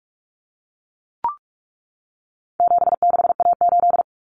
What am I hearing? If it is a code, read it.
Morse code, heard as a keyed tone, about 44 wpm, 700 Hz. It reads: 76A8
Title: Common Cut Numbers - 4-Digit 45wpm